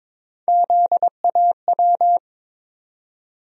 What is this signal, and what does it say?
Morse code, heard as a keyed tone, about 22 wpm, 700 Hz. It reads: ZAW
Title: USA Call Sign Postfixes - 3 Letters 22wpm